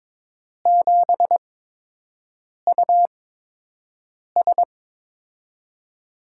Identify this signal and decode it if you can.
Morse code, heard as a keyed tone, about 22 wpm, spaced as for 6 wpm, 700 Hz. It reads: 7US
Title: USA Call Signs - 4 Characters - 8x Character Spacing - 22wpm